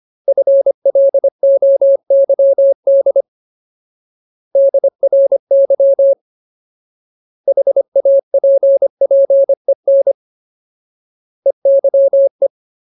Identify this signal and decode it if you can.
Morse code, heard as a keyed tone, about 25 wpm, 550 Hz. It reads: FLOYD DRY HAPPEN EYE
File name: Ultimate Word List / 25WPM Ultimate